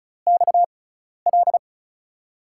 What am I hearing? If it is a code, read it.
Morse code, heard as a keyed tone, about 36 wpm, spaced as for 12 wpm, 700 Hz. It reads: XL
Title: USA Call Signs - 5 Characters - 6x Character Spacing - 35wpm